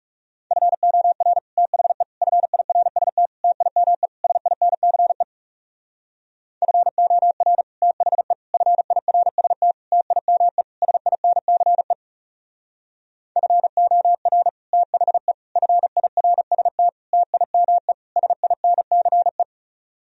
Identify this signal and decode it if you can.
Morse code, heard as a keyed tone, about 38 wpm, 700 Hz. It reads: FOR THE FIRST TIME SINCE FOR THE FIRST TIME SINCE FOR THE FIRST TIME SINCE